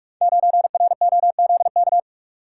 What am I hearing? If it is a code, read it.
Morse code, heard as a keyed tone, about 44 wpm, 700 Hz. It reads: 9ROZK
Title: Code Groups - 5 Letter:Numbers - 45wpm